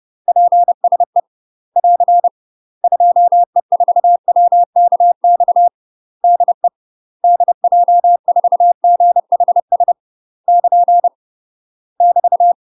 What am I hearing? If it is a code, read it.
Morse code, heard as a keyed tone, about 30 wpm, 700 Hz. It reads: PSE <AR> 2E4WKX DE DJ4GHS <KN> <BT>
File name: UK MASTER QSOs TRAINING/30WPM